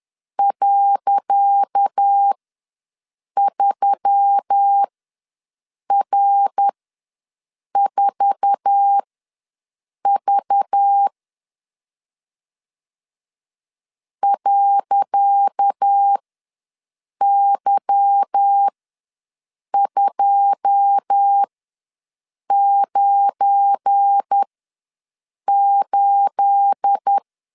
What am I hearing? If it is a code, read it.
Morse code, heard as a keyed tone, about 10 wpm, spaced as for 6 wpm, 800 Hz. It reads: .3R4V .Y298
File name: LEZIONI / 10 SIGNS / 10C DOT ws 10wpm 4213